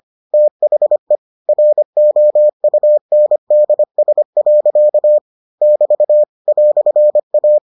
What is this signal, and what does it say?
Morse code, heard as a keyed tone, about 25 wpm, 600 Hz. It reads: THE ROUNDS. <BT> "A